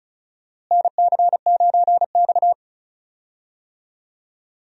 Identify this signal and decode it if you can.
Morse code, heard as a keyed tone, about 36 wpm, 700 Hz. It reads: NC9X
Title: USA Call Signs - 4 Characters - Repeated 3x - 35wpm